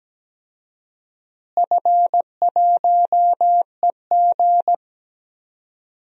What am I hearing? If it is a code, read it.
Morse code, heard as a keyed tone, about 17 wpm, 700 Hz. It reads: F1EG
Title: DXCC-100 Call Signs - 4 Characters - Repeated 2x - 17wpm